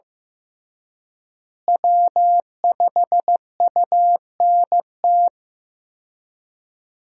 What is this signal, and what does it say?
Morse code, heard as a keyed tone, about 16 wpm, 700 Hz. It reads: W5UNT
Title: USA Call Signs - 5 Characters - Repeated 3x - 15wpm